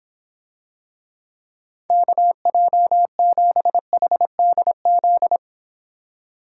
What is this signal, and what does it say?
Morse code, heard as a keyed tone, about 26 wpm, 700 Hz. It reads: KJ7HDZ